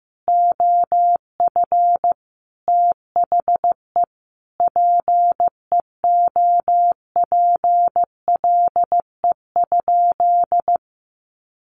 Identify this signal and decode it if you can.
Morse code, heard as a keyed tone, about 14 wpm, 700 Hz. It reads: OF THE PEOPLE?
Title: Sentences from Top 1500 Words 15wpm